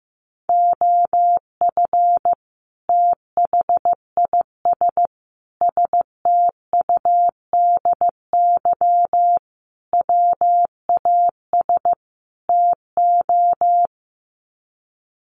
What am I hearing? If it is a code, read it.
Morse code, heard as a keyed tone, about 15 wpm, 700 Hz. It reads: OF THIS STUDY WAS TO
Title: Sets of 5 Words 15wpm